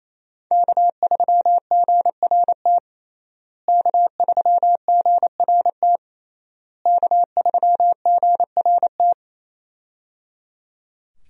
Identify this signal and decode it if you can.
Morse code, heard as a keyed tone, about 28 wpm, 700 Hz. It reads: K3GRT K3GRT K3GRT
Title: USA Call Signs - 5 Characters - Repeated 3x - 28wpm